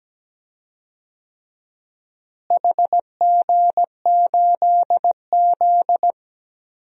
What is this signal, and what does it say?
Morse code, heard as a keyed tone, about 17 wpm, 700 Hz. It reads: HG8Z
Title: DXCC-100 Call Signs - 4 Characters - Repeated 3x - 17wpm